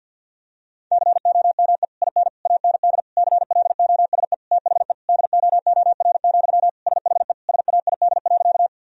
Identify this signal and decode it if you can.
Morse code, heard as a keyed tone, about 51 wpm, 700 Hz. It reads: COME IN AND CLOSE THE DOOR, SHE SAID.
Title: Sentences from Top 1000 Words - Review Part 1 - 50wpm